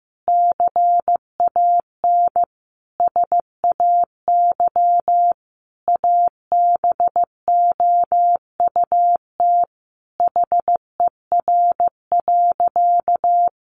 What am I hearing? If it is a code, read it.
Morse code, heard as a keyed tone, about 15 wpm, 700 Hz. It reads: CAN SAY ABOUT HER.
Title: Sentences from Top 200 Words 15wpm